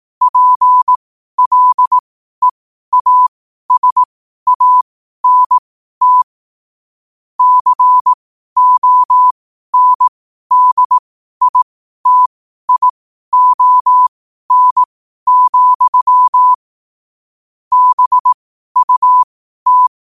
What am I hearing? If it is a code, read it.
Morse code, heard as a keyed tone, about 18 wpm, spaced as for 12 wpm, 1000 Hz. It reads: PLEASANT CONDITION, BUT
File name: qotdfiles / 12wpm